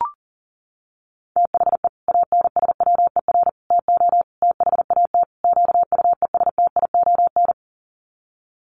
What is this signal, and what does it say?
Morse code, heard as a keyed tone, about 40 wpm, 700 Hz. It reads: THE ANSWER TO THAT QUESTION
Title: Sets of 5 Words Encore2 40wpm